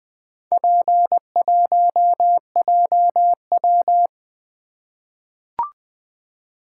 Morse code, heard as a keyed tone, about 20 wpm, 700 Hz. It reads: P1JW
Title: Four Letter-Number Combinations - 20wpm